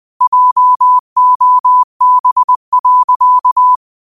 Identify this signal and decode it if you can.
Morse code, heard as a keyed tone, about 20 wpm, 1000 Hz. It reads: JOB.
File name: qotdfiles / 20wpm